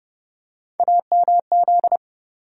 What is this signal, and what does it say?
Morse code, heard as a keyed tone, about 30 wpm, 700 Hz. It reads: AMZ
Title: USA Call Sign Postfixes - 3 Letters 30wpm